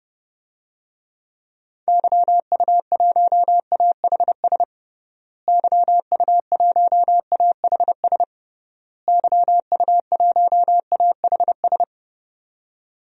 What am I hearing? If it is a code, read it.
Morse code, heard as a keyed tone, about 30 wpm, 700 Hz. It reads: YU1AHS YU1AHS YU1AHS
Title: DXCC-100 Call Signs - 6 Characters - Repeated 3x - 30wpm